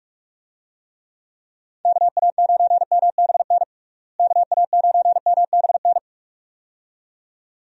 Morse code, heard as a keyed tone, about 43 wpm, 700 Hz. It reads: KA9MBN KA9MBN
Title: USA Call Signs - 6 Characters - Repeated 2x - 45wpm